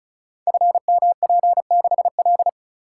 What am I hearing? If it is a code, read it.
Morse code, heard as a keyed tone, about 35 wpm, 700 Hz. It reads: FMP6L
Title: Code Groups - 5 Letter:Numbers - 35wpm